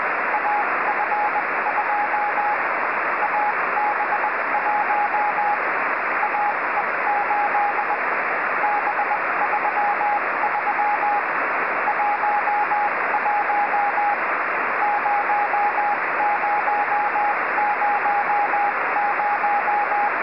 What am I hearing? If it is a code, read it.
Morse code, heard as a keyed tone, about 20 wpm, spaced as for 17 wpm, 830 Hz, with radio noise. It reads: AF2 AB1 AE8 B33 11 0000